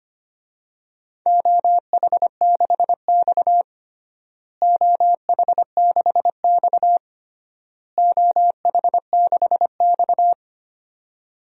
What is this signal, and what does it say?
Morse code, heard as a keyed tone, about 25 wpm, 700 Hz. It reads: OH6X OH6X OH6X
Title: DXCC-100 Call Signs - 4 Characters - Repeated 3x - 25wpm